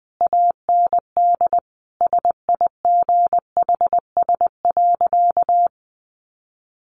Morse code, heard as a keyed tone, about 20 wpm, 700 Hz. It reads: AND SIGHS.
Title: A Sound Of Thunder 20wpm